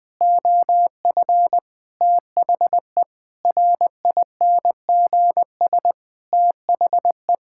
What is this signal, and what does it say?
Morse code, heard as a keyed tone, about 20 wpm, 700 Hz. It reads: OF THE RINGS THE
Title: Sets of 9 Words - 20wpm